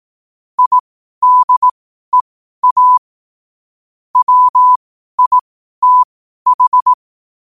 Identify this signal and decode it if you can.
Morse code, heard as a keyed tone, about 18 wpm, spaced as for 12 wpm, 1000 Hz. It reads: IDEA WITH
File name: qotdfiles / 12wpm